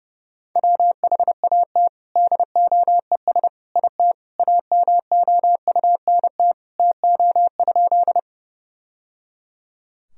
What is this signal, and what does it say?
Morse code, heard as a keyed tone, about 30 wpm, 700 Hz. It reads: WHAT DOES IT AMOUNT TO?
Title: Sentences from Top 1000 Words - Review Part 4 - 30wpm